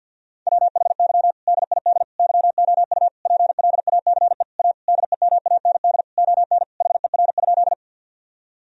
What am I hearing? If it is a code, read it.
Morse code, heard as a keyed tone, about 50 wpm, 700 Hz. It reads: WHY DID YOU PLACE A DEMAND ON HER?
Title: Sentences from Top 1000 Words - Review Part 4 - 50wpm